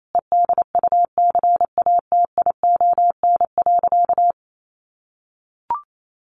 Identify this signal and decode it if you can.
Morse code, heard as a keyed tone, about 28 wpm, 700 Hz. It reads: EDUCATION.
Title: Sentences from Top 1500 Words 28wpm